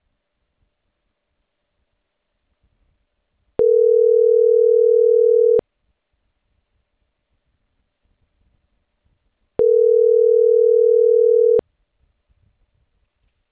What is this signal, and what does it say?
Ringback tone (North America), 440 + 480 Hz.